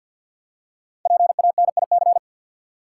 Morse code, heard as a keyed tone, about 51 wpm, 700 Hz. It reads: PANIC
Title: Five Letter Words - 50wpm